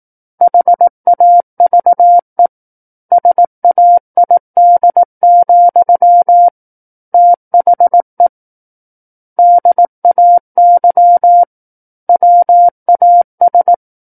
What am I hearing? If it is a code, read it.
Morse code, heard as a keyed tone, about 18 wpm, 700 Hz. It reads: HAVE SAID, THE DAY WAS
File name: 18 WPM